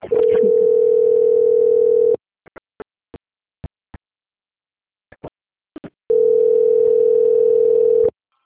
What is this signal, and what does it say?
Ringback tone (North America), 440 + 480 Hz.
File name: out- - - - - .30834